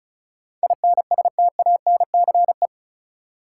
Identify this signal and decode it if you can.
Morse code, heard as a keyed tone, about 34 wpm, 700 Hz. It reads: INSTANCE
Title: Sentences from Top 800 Words - New Word Drill - 35wpm